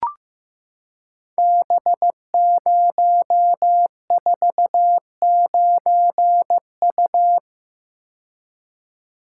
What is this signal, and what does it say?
Morse code, heard as a keyed tone, about 15 wpm, 700 Hz. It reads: B049U
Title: Code Groups - 5 Letter:Numbers - 15wpm